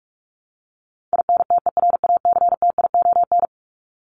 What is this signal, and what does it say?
Morse code, heard as a keyed tone, about 44 wpm, 700 Hz. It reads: INTERACTION
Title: Top 1251-1500 Words 45wpm